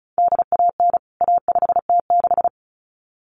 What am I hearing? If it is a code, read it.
Morse code, heard as a keyed tone, about 34 wpm, 700 Hz. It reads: DAN A5T6
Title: CWT Exchange 35wpm